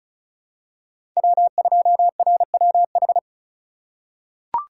Morse code, heard as a keyed tone, about 34 wpm, 700 Hz. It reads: W2RWH
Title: USA Call Signs - 5 Characters - 3x Character Spacing - 35wpm